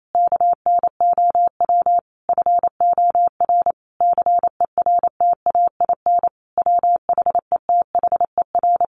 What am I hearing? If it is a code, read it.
Morse code, heard as a keyed tone, about 28 wpm, 700 Hz. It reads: KNOW FOR CERTAIN WHETHER
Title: A Sound Of Thunder 28wpm